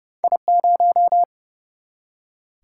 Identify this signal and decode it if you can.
Morse code, heard as a keyed tone, about 29 wpm, 700 Hz. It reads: I0
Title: Single-Letter Single-Number 30wpm